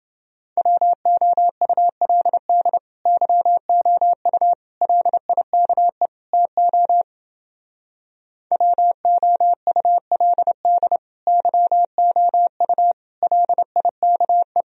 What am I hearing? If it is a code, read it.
Morse code, heard as a keyed tone, about 29 wpm, 700 Hz. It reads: WOULD YOU LIKE TO WOULD YOU LIKE